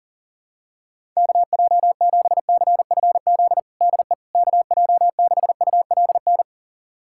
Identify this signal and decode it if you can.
Morse code, heard as a keyed tone, about 40 wpm, 700 Hz. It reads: KJ7CFZ DE KJ6ULN
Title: USA Callsign DE Callsign - 6 Characters - 40wpm